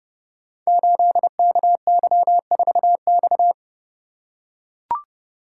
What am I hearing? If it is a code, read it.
Morse code, heard as a keyed tone, about 30 wpm, 700 Hz. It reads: 8KY4X
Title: Code Groups - 5 Letter:Numbers - 30wpm